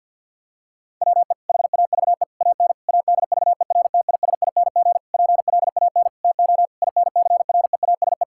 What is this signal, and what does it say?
Morse code, heard as a keyed tone, about 47 wpm, 700 Hz. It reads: WE HAVE AN ADVERTISING PLAN TO INCREASE
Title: Sentences from Top 1000 Words - Review Part 6 - 50wpm